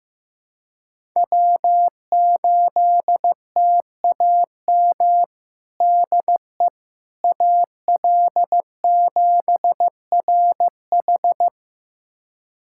Morse code, heard as a keyed tone, about 15 wpm, 700 Hz. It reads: W8TAM DE AL7RH
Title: USA Callsign DE Callsign - 5 Characters - 15wpm